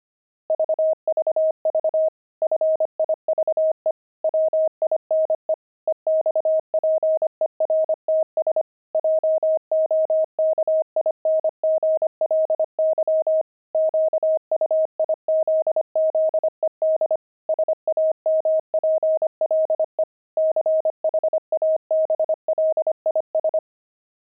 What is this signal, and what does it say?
Morse code, heard as a keyed tone, about 25 wpm, 630 Hz. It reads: VVV FIVE WINE EXPERTS JOKINGLY QUIZZED SAMPLE CHABLIS
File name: pangram-fs-25-cs-25-wine